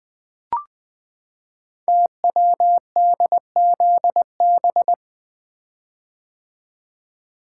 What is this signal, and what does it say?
Morse code, heard as a keyed tone, about 20 wpm, 700 Hz. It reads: TWDZB
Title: Code Groups - 5 Letters - 20wpm